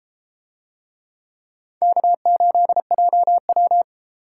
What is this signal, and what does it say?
Morse code, heard as a keyed tone, about 32 wpm, 700 Hz. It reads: K8JW